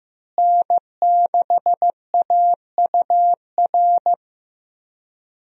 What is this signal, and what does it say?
Morse code, heard as a keyed tone, about 16 wpm, 700 Hz. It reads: N6AUR
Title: USA Call Signs - 5 Characters - Repeated 3x - 15wpm